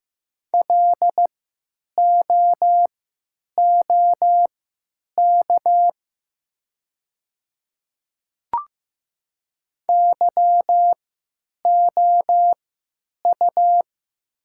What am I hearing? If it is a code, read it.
Morse code, heard as a keyed tone, about 15 wpm, spaced as for 9 wpm, 700 Hz. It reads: LOOK YOU
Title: 3x Character Spacing - Top 200 Words - 15wpm